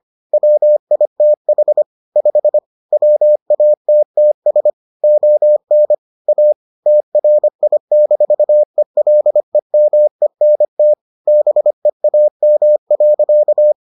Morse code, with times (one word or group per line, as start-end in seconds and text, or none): 0.3-1.8 WITH
2.2-2.6 5
2.9-4.7 WATTS
5.0-5.9 ON
6.3-6.5 A
6.9-10.9 TRI-ELEMENT
11.3-13.7 BEAM.